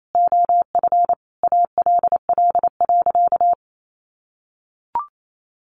Morse code, heard as a keyed tone, about 28 wpm, 700 Hz. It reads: OF ALL.
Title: Sentences from Top 1400 Words 28wpm